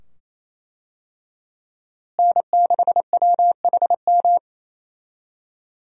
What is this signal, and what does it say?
Morse code, heard as a keyed tone, about 27 wpm, 700 Hz. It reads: N6WHM
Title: USA Call Signs - 5 Characters - Repeated 2x - 28wpm